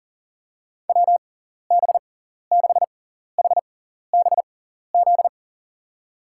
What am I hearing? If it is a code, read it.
Morse code, heard as a keyed tone, about 40 wpm, spaced as for 14 wpm, 700 Hz. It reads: WB6HBZ
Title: USA Call Signs - 6 Characters - 6x Character Spacing - 40wpm